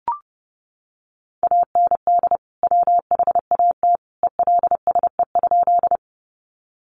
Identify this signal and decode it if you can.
Morse code, heard as a keyed tone, about 28 wpm, 700 Hz. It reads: AND WHAT ELSE?
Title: Sentences from Top 800 Words 30wpm